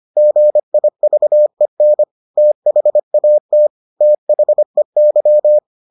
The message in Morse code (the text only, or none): GIVEN THAT THEY